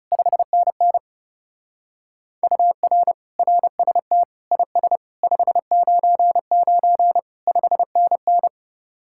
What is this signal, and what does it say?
Morse code, heard as a keyed tone, about 37 wpm, 700 Hz. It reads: 5NN UR RST IS 599 5NN
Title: Speed-Racing Common QSO Phrases - 30wpm